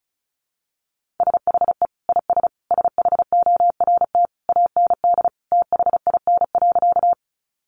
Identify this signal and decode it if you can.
Morse code, heard as a keyed tone, about 34 wpm, 700 Hz. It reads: SHE IS SHORT AND THIN.
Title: Sentences from Top 1200 Words 35wpm